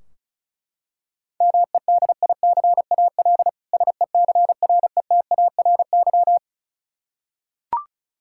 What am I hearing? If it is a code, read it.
Morse code, heard as a keyed tone, about 36 wpm, 700 Hz. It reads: MEDICAL SECRETARY
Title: Practice QSO - Exchange 3 - More Info - 35wpm